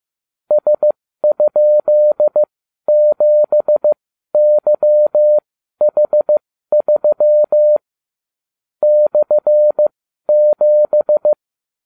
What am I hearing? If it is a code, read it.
Morse code, heard as a keyed tone, about 15 wpm, spaced as for 12 wpm, 600 Hz. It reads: S?7YH3 /7